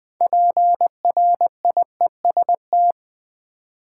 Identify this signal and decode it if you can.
Morse code, heard as a keyed tone, about 20 wpm, 700 Hz. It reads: PRIEST
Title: Six Letter Words - 20wpm